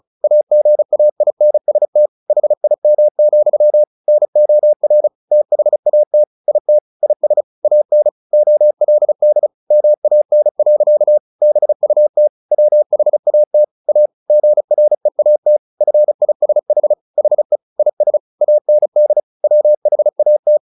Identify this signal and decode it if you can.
Morse code, heard as a keyed tone, about 34 wpm, 600 Hz. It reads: AGAINST HIM, NOR THAT IT IS AN OLD MAN. BUT WHAT A GREAT FISH HE IS AND WHAT
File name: the old man and the sea / the old man and the sea 1 35wpm